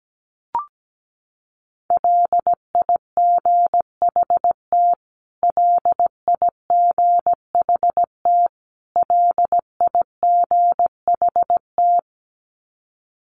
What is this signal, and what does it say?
Morse code, heard as a keyed tone, about 17 wpm, 700 Hz. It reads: LIGHT LIGHT LIGHT
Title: Top 500 Words Repeated 17wpm